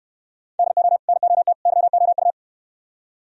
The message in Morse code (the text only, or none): DO NOT YOU